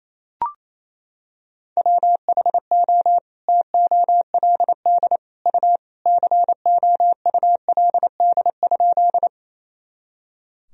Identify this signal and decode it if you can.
Morse code, heard as a keyed tone, about 27 wpm, 700 Hz. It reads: WHO TOLD U COULD?
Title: Sentences from Top 400 Words and Abbreviations 28wpm